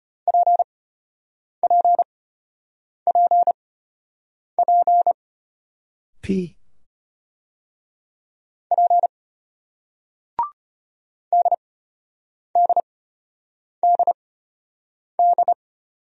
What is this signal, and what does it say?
Morse code, heard as a keyed tone, about 34 wpm, spaced as for 8 wpm, 700 Hz. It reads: PPPP P DDDD